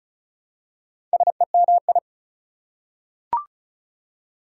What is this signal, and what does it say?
Morse code, heard as a keyed tone, about 36 wpm, 700 Hz. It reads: SEMI
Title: Prefixes and Suffixes 35wpm